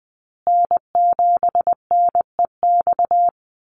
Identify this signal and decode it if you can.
Morse code, heard as a keyed tone, about 20 wpm, 700 Hz. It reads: N7NEX
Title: USA Call Signs Repeated 2x 20wpm